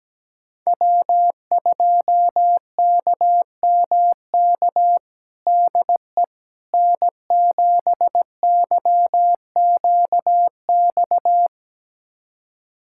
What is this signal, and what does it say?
Morse code, heard as a keyed tone, about 17 wpm, 700 Hz. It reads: W2KMK DE N7YQX
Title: USA Callsign DE Callsign - 5 Characters - 17wpm